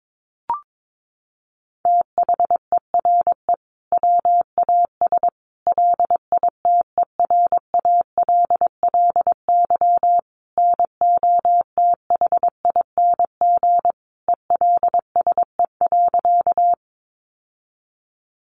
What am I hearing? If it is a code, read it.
Morse code, heard as a keyed tone, about 22 wpm, 700 Hz. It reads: THERE WAS LITERALLY NOTHING ELSE.
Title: Sentences from Top 1400 Words 22wpm